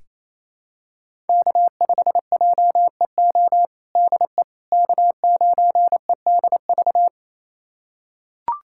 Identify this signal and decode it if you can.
Morse code, heard as a keyed tone, about 28 wpm, 700 Hz. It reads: K5JEO DE K9EDV
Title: USA Callsign DE Callsign - 5 Characters - 28wpm